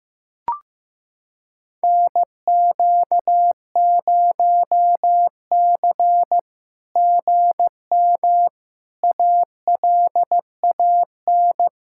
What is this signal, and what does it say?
Morse code, heard as a keyed tone, about 16 wpm, 700 Hz. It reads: NQ0C GM ALAN
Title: SOTA QSO Practice 15wpm